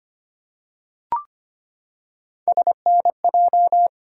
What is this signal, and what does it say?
Morse code, heard as a keyed tone, about 25 wpm, 700 Hz. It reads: SNJ
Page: morse-code-ninja-practice-25wpm